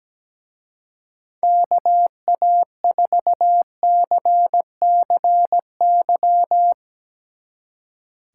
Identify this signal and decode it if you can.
Morse code, heard as a keyed tone, about 17 wpm, 700 Hz. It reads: KA4CCY
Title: Speed Racing - USA 2x3 Call Signs 17wpm